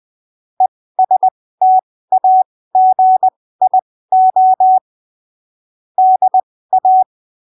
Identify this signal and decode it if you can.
Morse code, heard as a keyed tone, about 20 wpm, spaced as for 15 wpm, 750 Hz. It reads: ESTAGIO DA
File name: arquivos / texto / texto 05 20ppm